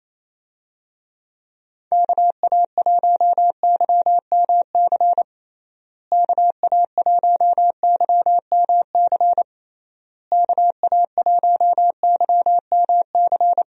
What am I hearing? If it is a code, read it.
Morse code, heard as a keyed tone, about 27 wpm, 700 Hz. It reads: KA1YMC KA1YMC KA1YMC
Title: USA Call Signs - 6 Characters - Repeated 3x - 28wpm